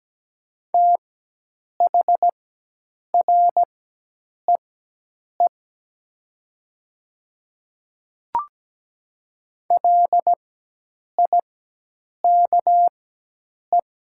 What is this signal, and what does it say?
Morse code, heard as a keyed tone, about 17 wpm, spaced as for 8 wpm, 690 Hz. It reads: THREE LIKE
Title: 4x Character Spacing - Top 200 Words - 17wpm